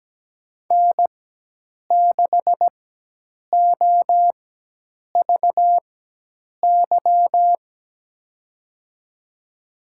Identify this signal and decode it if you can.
Morse code, heard as a keyed tone, about 17 wpm, spaced as for 8 wpm, 700 Hz. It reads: N6OVY
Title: USA Call Signs - 5 Characters - 4x Character Spacing - 17wpm